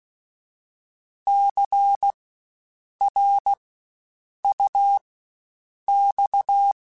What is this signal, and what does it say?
Morse code, heard as a keyed tone, about 16 wpm, spaced as for 7 wpm, 800 Hz. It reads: CRUX
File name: Spaced Alphas / Spaced Alphas mp3s / Cs sp @15